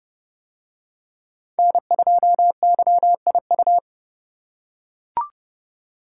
Morse code, heard as a keyed tone, about 28 wpm, 700 Hz. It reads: N2YIU